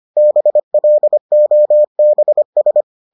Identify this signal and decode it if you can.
Morse code, heard as a keyed tone, about 25 wpm, 600 Hz. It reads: BLOBS